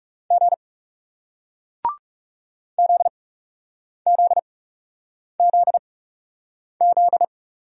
Morse code, heard as a keyed tone, about 40 wpm, spaced as for 8 wpm, 700 Hz. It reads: G ZZZZ